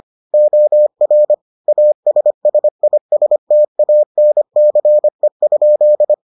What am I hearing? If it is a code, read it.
Morse code, heard as a keyed tone, about 25 wpm, 600 Hz. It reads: OR ASSISTANCE?